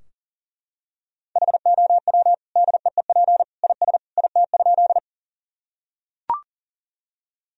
Morse code, heard as a keyed tone, about 40 wpm, 700 Hz. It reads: HOW DEEP IS IT?
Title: Sentences from Top 1000 Words - Review Part 3 - 40wpm